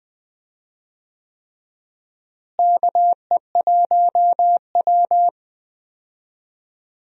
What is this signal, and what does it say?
Morse code, heard as a keyed tone, about 20 wpm, 700 Hz. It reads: KE1W
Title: USA Call Signs - 4 Characters - Repeated 3x - 20wpm